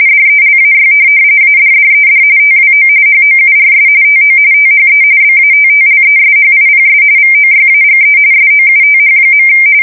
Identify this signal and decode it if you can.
RTTY, about 45.45 baud. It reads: DELAY ADDED TWO HOURS TO THE BATTLESHIPS VOYAGE SOUTH.
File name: fsk170 / USS-Massachusetts